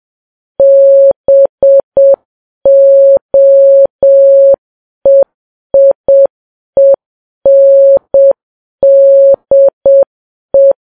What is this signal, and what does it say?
Morse code, heard as a keyed tone, about 7 wpm, 550 Hz. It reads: BOEIENDE
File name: Files Web tls / Morse on9fv / morse7kps